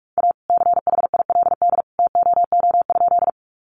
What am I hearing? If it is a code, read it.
Morse code, heard as a keyed tone, about 44 wpm, 700 Hz. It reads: A CHILD TOO?
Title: Sentences from Top 800 Words 45wpm